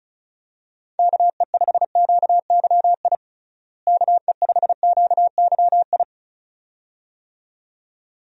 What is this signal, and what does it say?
Morse code, heard as a keyed tone, about 34 wpm, 700 Hz. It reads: KE5QYI KE5QYI
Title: USA Call Signs - 6 Characters - Repeated 2x - 35wpm